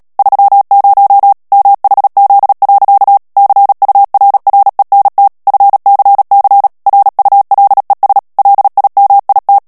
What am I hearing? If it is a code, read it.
Morse code, heard as a keyed tone, about 37 wpm, 800 Hz. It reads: 30 MHZ. CURRENT FCC RULES LIMIT
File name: morse mp3 / ANGLAIS texte clair 3 vitesse 35 mots minute